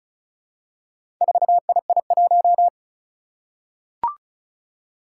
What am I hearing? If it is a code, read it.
Morse code, heard as a keyed tone, about 36 wpm, 700 Hz. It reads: 4II1
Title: Four Challenging Letter-Number Combinations 35wpm